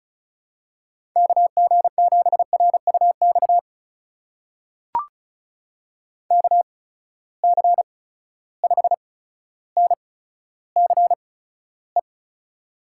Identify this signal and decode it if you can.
Morse code, heard as a keyed tone, about 36 wpm, spaced as for 15 wpm, 700 Hz. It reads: KG7RUX KC5NCE
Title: USA Call Signs - 6 Characters - 8x Character Spacing - 35wpm